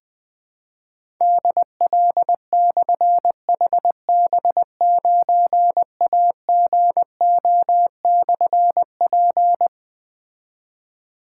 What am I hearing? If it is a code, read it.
Morse code, heard as a keyed tone, about 20 wpm, 700 Hz. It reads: DL/HB9AGO/P
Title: Difficult Call Signs 20wpm